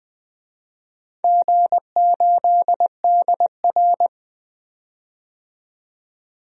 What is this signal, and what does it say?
Morse code, heard as a keyed tone, about 21 wpm, 700 Hz. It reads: G8DR
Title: DXCC-100 Call Signs - 4 Characters - 20wpm